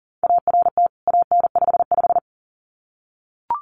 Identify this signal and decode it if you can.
Morse code, heard as a keyed tone, about 40 wpm, 700 Hz. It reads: ART AN55
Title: CWT Exchange 40wpm